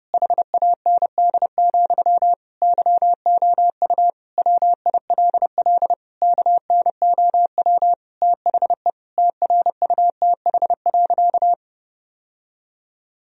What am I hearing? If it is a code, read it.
Morse code, heard as a keyed tone, about 30 wpm, 700 Hz. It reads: HAND, YOU WILL KNOW THE TRUTH.
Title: Sentences from Top 1000 Words - Review Part 3 - 30wpm